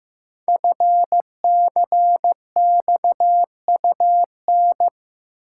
Morse code, heard as a keyed tone, about 15 wpm, 700 Hz. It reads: FCXUN
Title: Code Groups - 5 Letters - 15wpm